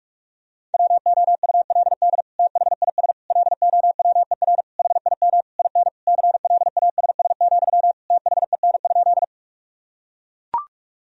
Morse code, heard as a keyed tone, about 41 wpm, 700 Hz. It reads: WOULD THIS LOWER HIM IN CLASS, THEN?
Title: Sentences from Top 1000 Words - Review Part 4 - 45wpm